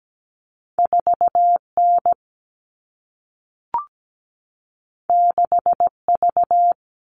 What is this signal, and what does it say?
Morse code, heard as a keyed tone, about 17 wpm, 700 Hz. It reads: 4N 6V
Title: All Cut Numbers - 2-Digit 17wpm